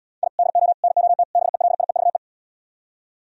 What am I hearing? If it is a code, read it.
Morse code, heard as a keyed tone, about 77 wpm, 700 Hz. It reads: I DO NOT BELIEVE